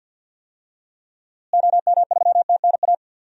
Morse code, heard as a keyed tone, about 48 wpm, 700 Hz. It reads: OM3TNA
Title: DXCC-100 Call Signs - 6 Characters - Repeated 3x - 50wpm